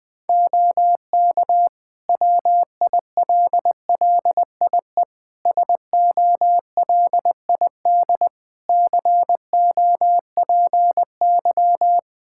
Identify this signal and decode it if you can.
Morse code, heard as a keyed tone, about 20 wpm, 700 Hz. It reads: OK WILLIE SOLID COPY
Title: Practice QSO - Exchange 3 - More Info - 20wpm